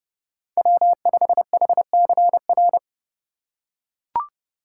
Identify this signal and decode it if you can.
Morse code, heard as a keyed tone, about 31 wpm, 700 Hz. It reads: W5HCR
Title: USA Call Signs - 2x Character Spacing - 30wpm